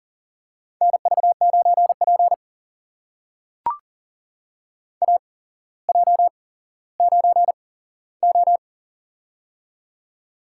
Morse code, heard as a keyed tone, about 42 wpm, spaced as for 17 wpm, 700 Hz. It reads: NV9P AJ9O
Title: USA Call Signs - 4 Characters - 8x Character Spacing - 40wpm